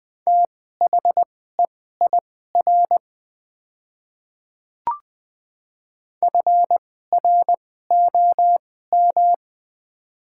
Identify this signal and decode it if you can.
Morse code, heard as a keyed tone, about 20 wpm, spaced as for 14 wpm, 700 Hz. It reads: THEIR FROM
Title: 2x Character Spacing - Top 200 Words - 20wpm